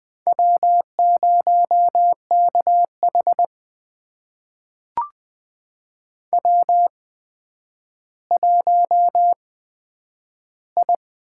Morse code, heard as a keyed tone, about 20 wpm, spaced as for 10 wpm, 700 Hz. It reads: W0KH W1I
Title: USA Call Signs - 4 Characters - 8x Character Spacing - 20wpm